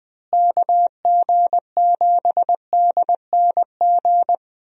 Morse code, heard as a keyed tone, about 21 wpm, 700 Hz. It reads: KG7DNG
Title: USA Call Signs - 6 Characters - Repeated 2x - 20wpm